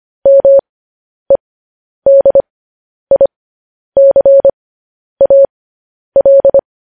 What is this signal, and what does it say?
Morse code, heard as a keyed tone, about 25 wpm, spaced as for 10 wpm, 550 Hz. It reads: MEDICAL
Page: News Headlines in Morse Code at 10 WPM